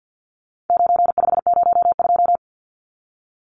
Morse code, heard as a keyed tone, about 50 wpm, 700 Hz. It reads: 9502
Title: Numbers - 4-Digit 50wpm